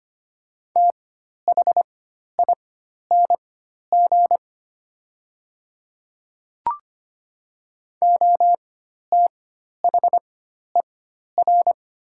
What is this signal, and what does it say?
Morse code, heard as a keyed tone, about 25 wpm, spaced as for 12 wpm, 700 Hz. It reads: THING OTHER
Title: 4x Character Spacing - Top 200 Words - 25wpm